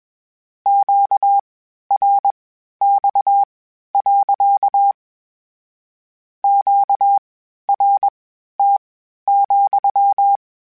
Morse code, heard as a keyed tone, about 21 wpm, spaced as for 12 wpm, 800 Hz. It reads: QRX. QRT,
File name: Q Calls / Q Calls-mp3s / Q Calls sppc @20